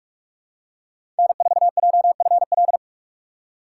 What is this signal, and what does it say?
Morse code, heard as a keyed tone, about 46 wpm, 700 Hz. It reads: N4JFL
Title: USA Call Signs - 5 Characters - 4x Character Spacing - 45wpm